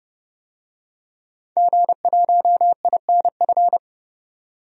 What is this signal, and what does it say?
Morse code, heard as a keyed tone, about 32 wpm, 700 Hz. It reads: G1INF
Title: DXCC-100 Call Signs - 5 Characters - 30wpm